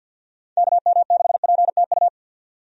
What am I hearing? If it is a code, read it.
Morse code, heard as a keyed tone, about 50 wpm, 700 Hz. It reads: KM6PTU
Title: USA Call Signs - 6 Characters - 50wpm